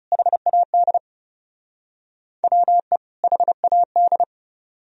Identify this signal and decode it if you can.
Morse code, heard as a keyed tone, about 37 wpm, 700 Hz. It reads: HAD WE HAD
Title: Speed-Racing - Sets of 2 Words 30wpm